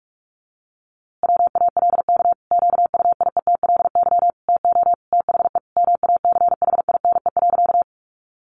Morse code, heard as a keyed tone, about 44 wpm, 700 Hz. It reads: WALK QUIETLY TO THE MACHINE.
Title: A Sound Of Thunder 45wpm